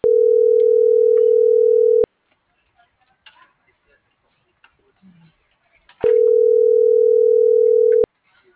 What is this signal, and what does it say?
Ringback tone (North America), 440 + 480 Hz.